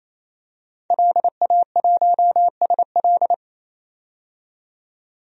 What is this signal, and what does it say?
Morse code, heard as a keyed tone, about 28 wpm, 700 Hz. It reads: LA1SL
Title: DXCC-100 Call Signs - 5 Characters - 28wpm